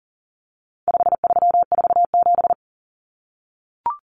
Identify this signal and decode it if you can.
Morse code, heard as a keyed tone, about 40 wpm, 700 Hz. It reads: 5347
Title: Common Cut Numbers - 4-Digit 40wpm